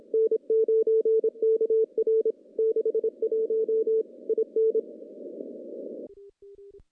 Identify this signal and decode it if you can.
Morse code, heard as a keyed tone, about 26 wpm, 440 Hz. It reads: N9KR 61 IN
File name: contests / CW-SS-k5tr- -0200